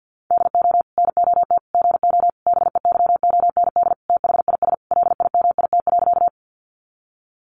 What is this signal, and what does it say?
Morse code, heard as a keyed tone, about 50 wpm, 700 Hz. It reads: DO NOT GO BEYOND THIS LIMIT.
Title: Sentences from Top 1500 Words 50wpm